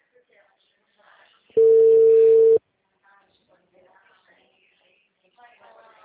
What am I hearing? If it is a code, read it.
Ringback tone (China, 450 Hz), with speech.
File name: out- -2005- - - .96